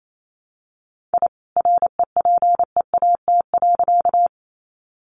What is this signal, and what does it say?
Morse code, heard as a keyed tone, about 28 wpm, 700 Hz. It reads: I REPEAT.
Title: A Sound Of Thunder 28wpm